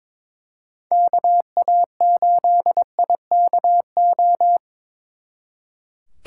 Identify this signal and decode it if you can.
Morse code, heard as a keyed tone, about 26 wpm, 700 Hz. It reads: KA8IKO
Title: Speed Racing - USA 2x3 Call Signs 22wpm